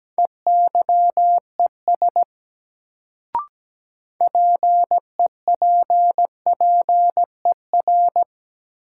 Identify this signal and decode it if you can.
Morse code, heard as a keyed tone, about 17 wpm, 700 Hz. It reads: EYES PEPPER
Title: Sets of 5 Words - New Word Drill - 17wpm